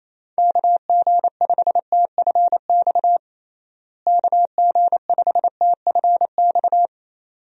Morse code, heard as a keyed tone, about 28 wpm, 700 Hz. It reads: KG5TFX KG5TFX
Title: USA Call Signs - 6 Characters - Repeated 2x - 28wpm